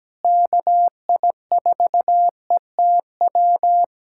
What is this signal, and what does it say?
Morse code, heard as a keyed tone, about 17 wpm, 700 Hz. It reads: KI4ETW
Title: USA Callsign DE Callsign - 6 Characters - 17wpm